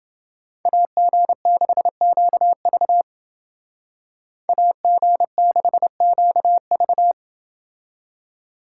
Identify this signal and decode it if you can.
Morse code, heard as a keyed tone, about 29 wpm, 700 Hz. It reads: AG6QV AG6QV